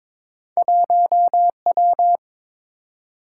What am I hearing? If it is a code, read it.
Morse code, heard as a keyed tone, about 23 wpm, 700 Hz. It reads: 1W
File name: practice / 22wpm / Two Letter-Number Combinations - 22wpm